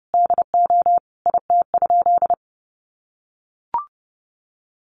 Morse code, heard as a keyed tone, about 30 wpm, 700 Hz. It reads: DO IT?
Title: Sentences from Top 1200 Words 30wpm